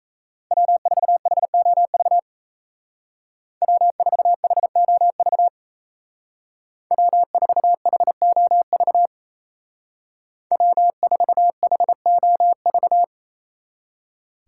Morse code, heard as a keyed tone, about 36 wpm, 700 Hz. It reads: W4HOV W4HOV W4HOV W4HOV